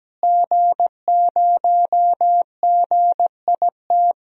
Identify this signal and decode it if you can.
Morse code, heard as a keyed tone, about 17 wpm, 700 Hz. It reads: G0GIT
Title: DXCC-100 Call Signs - 5 Characters - Repeated 2x - 17wpm